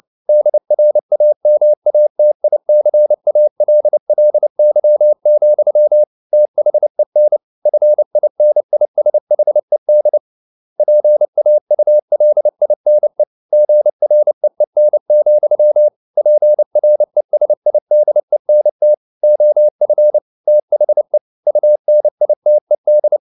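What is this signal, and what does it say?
Morse code, heard as a keyed tone, about 29 wpm, 600 Hz. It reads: DRAMATICALLY, THEN FINISHED PAULINE GREEN, PRESIDENT OF THE UNITED
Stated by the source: Morse code at 29 wpm, sending Ken Follett's novel Never in English, from CW med Gnister.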